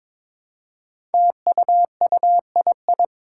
Morse code, heard as a keyed tone, about 22 wpm, 700 Hz. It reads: TUUII
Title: Five Challenging Letter-Number Combinations 22wpm